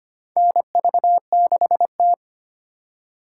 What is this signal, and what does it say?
Morse code, heard as a keyed tone, about 25 wpm, 700 Hz. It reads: NV6T
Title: USA Call Signs - 4 Characters - 4x Character Spacing - 25wpm